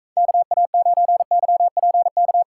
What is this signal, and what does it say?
Morse code, heard as a keyed tone, about 44 wpm, 700 Hz. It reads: KA9YPK